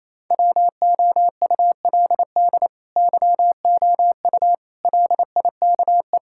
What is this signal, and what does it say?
Morse code, heard as a keyed tone, about 28 wpm, 700 Hz. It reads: WOULD YOU LIKE
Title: Sentences from Top 1000 Words - Review Part 5 - 28wpm